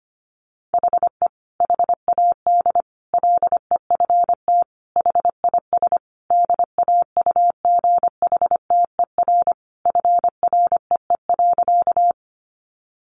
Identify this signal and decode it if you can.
Morse code, heard as a keyed tone, about 25 wpm, 700 Hz. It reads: HE HAD LEFT HIS DAUGHTER FREE.
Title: Sentences from Top 1500 Words 25wpm